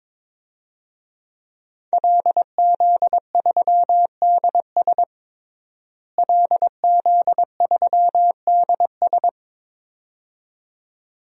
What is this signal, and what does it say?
Morse code, heard as a keyed tone, about 22 wpm, 700 Hz. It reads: LZ3DS LZ3DS
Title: DXCC-100 Call Signs - 5 Characters - Repeated 2x - 22wpm